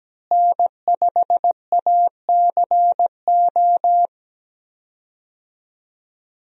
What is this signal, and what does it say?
Morse code, heard as a keyed tone, about 17 wpm, 700 Hz. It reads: N5ACO
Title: USA Call Signs - 5 Characters - Repeated 3x - 17wpm